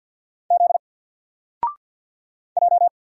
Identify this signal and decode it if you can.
Morse code, heard as a keyed tone, about 50 wpm, 700 Hz. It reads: Z J
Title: Single Letter-Number 50wpm